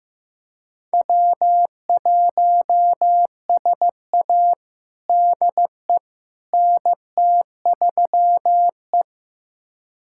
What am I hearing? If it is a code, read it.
Morse code, heard as a keyed tone, about 14 wpm, 700 Hz. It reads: W1SA DE NT3E
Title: USA Callsign DE Callsign - 4 Characters - 15wpm